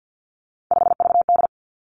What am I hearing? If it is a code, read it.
Morse code, heard as a keyed tone, about 50 wpm, 700 Hz. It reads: 5VD
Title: Three Challenging Letter-Number Combinations 50wpm